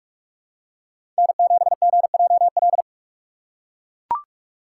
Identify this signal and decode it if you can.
Morse code, heard as a keyed tone, about 44 wpm, 700 Hz. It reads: N7GJL
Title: USA Call Signs - 5 Characters - 6x Character Spacing - 45wpm